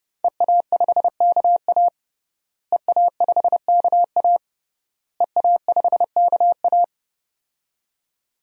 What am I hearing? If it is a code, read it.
Morse code, heard as a keyed tone, about 30 wpm, 700 Hz. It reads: EA5KA EA5KA EA5KA
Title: DXCC-100 Call Signs - 5 Characters - Repeated 3x - 30wpm